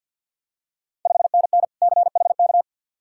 Morse code, heard as a keyed tone, about 53 wpm, 690 Hz. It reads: 5NN CHK
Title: Ontario QSO Party Exchange 50wpm